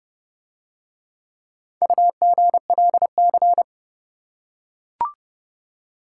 Morse code, heard as a keyed tone, about 26 wpm, 700 Hz. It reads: UGLC